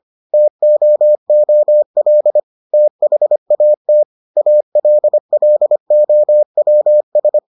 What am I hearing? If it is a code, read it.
Morse code, heard as a keyed tone, about 25 wpm, 600 Hz. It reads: TOOL THAT ALLOWS